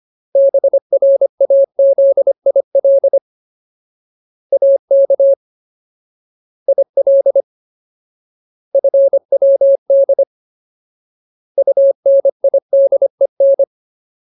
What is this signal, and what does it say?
Morse code, heard as a keyed tone, about 25 wpm, 550 Hz. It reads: BRAZIL AK IL FWD UNIDEN
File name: Ultimate Word List / 25WPM Ultimate